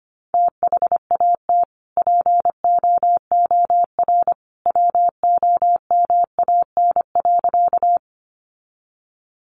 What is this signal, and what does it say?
Morse code, heard as a keyed tone, about 25 wpm, 690 Hz. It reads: THAT POOR WOMAN.
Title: Sentences from Top 1100 Words 25wpm